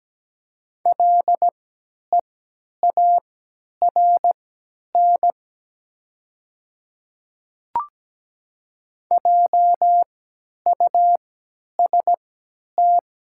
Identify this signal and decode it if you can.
Morse code, heard as a keyed tone, about 17 wpm, spaced as for 10 wpm, 700 Hz. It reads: LEARN JUST
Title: 3x Character Spacing - Top 300 Words - 17wpm